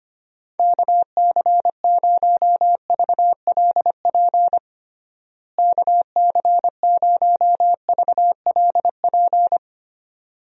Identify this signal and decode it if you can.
Morse code, heard as a keyed tone, about 25 wpm, 700 Hz. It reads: KC0VLP KC0VLP
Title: USA Call Signs - 6 Characters - Repeated 2x - 25wpm